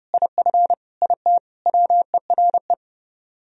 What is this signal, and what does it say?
Morse code, heard as a keyed tone, about 29 wpm, 700 Hz. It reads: IF IT WERE